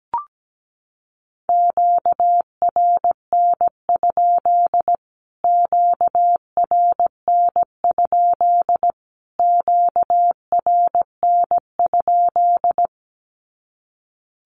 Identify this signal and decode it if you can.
Morse code, heard as a keyed tone, about 17 wpm, 700 Hz. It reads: QRN? QRN? QRN?
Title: Q-Codes Repeated 3x 17wpm